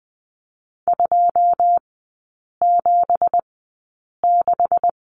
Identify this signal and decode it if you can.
Morse code, heard as a keyed tone, about 20 wpm, spaced as for 8 wpm, 700 Hz. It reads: 276
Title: 166 Lesson 27e - Number 7 - Three Characters 20wpm